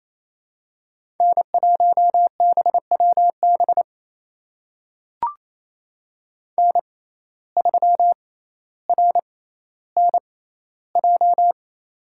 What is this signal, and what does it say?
Morse code, heard as a keyed tone, about 28 wpm, spaced as for 14 wpm, 700 Hz. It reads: N1BWB N3RNJ
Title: USA Call Signs - 5 Characters - 6x Character Spacing - 28wpm